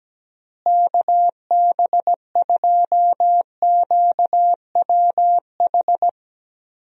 Morse code, heard as a keyed tone, about 17 wpm, 700 Hz. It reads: KB2QWH
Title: USA Call Signs - 6 Characters - Repeated 3x - 17wpm